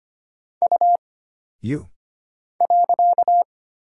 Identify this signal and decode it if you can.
Morse code, heard as a keyed tone, about 25 wpm, 700 Hz. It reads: U .
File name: Single Character (Letter:Number:Punctuation:Prosign) - Mind-Melt - 25wpm